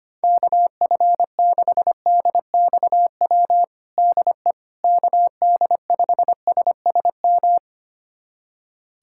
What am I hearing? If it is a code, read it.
Morse code, heard as a keyed tone, about 25 wpm, 700 Hz. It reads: KF6DXW DE KD5SSM
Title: USA Callsign DE Callsign - 6 Characters - 25wpm